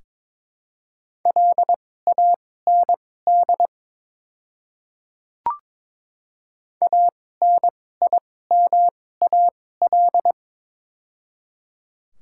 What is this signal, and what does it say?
Morse code, heard as a keyed tone, about 22 wpm, spaced as for 16 wpm, 700 Hz. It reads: LAND ANIMAL
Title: 2x Character Spacing - Top 300 Words - 22wpm